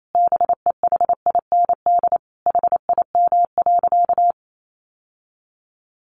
Morse code, heard as a keyed tone, about 28 wpm, 700 Hz. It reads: BEHIND HIM.
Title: Sentences from Top 1400 Words 28wpm